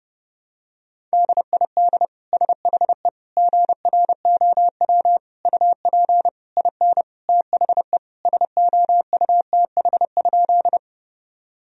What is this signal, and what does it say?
Morse code, heard as a keyed tone, about 32 wpm, 700 Hz. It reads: DID SHE GROW UP IN THE SOUTH?
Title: Sentences from Top 1000 Words - Review Part 2 - 30wpm